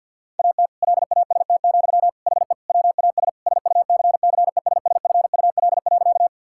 Morse code, heard as a keyed tone, about 50 wpm, 700 Hz. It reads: AT LAST, HE WAS SUCCESSFUL.
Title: Sentences from Top 1000 Words - Review Part 6 - 50wpm